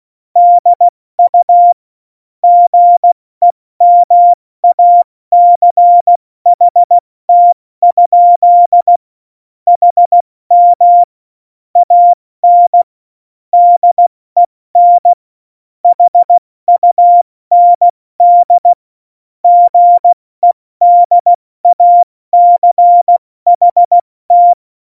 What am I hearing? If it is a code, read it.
Morse code, heard as a keyed tone, about 16 wpm, spaced as for 14 wpm, 700 Hz. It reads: DU GEMACHT? HM AN DEN HUND GEDACHT
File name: Morsekurs / Witze / Witz 19. September 17, Tempo 70